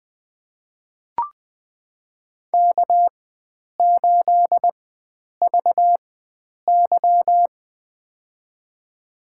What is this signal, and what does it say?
Morse code, heard as a keyed tone, about 20 wpm, spaced as for 9 wpm, 700 Hz. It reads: K8VY
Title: USA Call Signs - 4 Characters - 4x Character Spacing - 20wpm